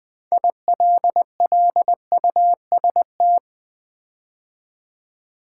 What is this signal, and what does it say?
Morse code, heard as a keyed tone, about 20 wpm, 700 Hz. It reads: ILLUST
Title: Build A Word - 6 Letters At A Time - 12-Letter Words - 20wpm